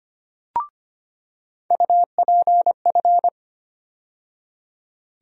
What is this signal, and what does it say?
Morse code, heard as a keyed tone, about 25 wpm, 700 Hz. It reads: UPF
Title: USA Call Sign Postfixes - 3 Letters 25wpm